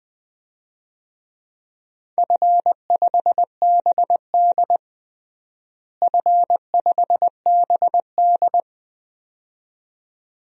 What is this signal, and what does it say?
Morse code, heard as a keyed tone, about 21 wpm, 700 Hz. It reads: F5BD F5BD
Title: DXCC-100 Call Signs - 4 Characters - Repeated 2x - 20wpm